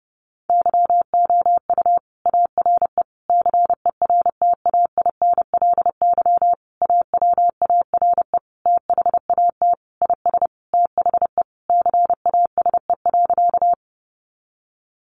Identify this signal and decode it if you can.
Morse code, heard as a keyed tone, about 30 wpm, 700 Hz. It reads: YOU ARE CERTAINLY AWARE THAT IS THE CASE.
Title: Sentences from Top 1200 Words 30wpm